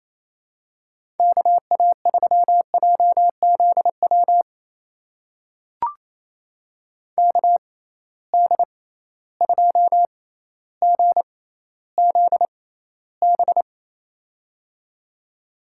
Morse code, heard as a keyed tone, about 28 wpm, spaced as for 14 wpm, 700 Hz. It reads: KA3JZW KD2GZB
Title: USA Call Signs - 6 Characters - 6x Character Spacing - 28wpm